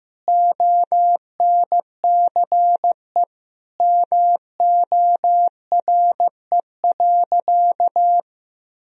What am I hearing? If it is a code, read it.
Morse code, heard as a keyed tone, about 15 wpm, 700 Hz. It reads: ONCE MORE.
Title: Sentences from Top 500 Words and Abbreviations 15wpm